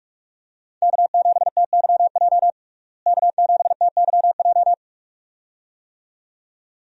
Morse code, heard as a keyed tone, about 41 wpm, 700 Hz. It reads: K7TYJ K7TYJ
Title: USA Call Signs - 5 Characters - Repeated 2x - 45wpm